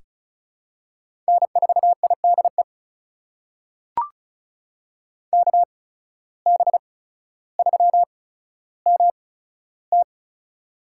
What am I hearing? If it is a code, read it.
Morse code, heard as a keyed tone, about 34 wpm, spaced as for 15 wpm, 700 Hz. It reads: N4IDE KB3MT
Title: USA Call Signs - 5 Characters - 8x Character Spacing - 35wpm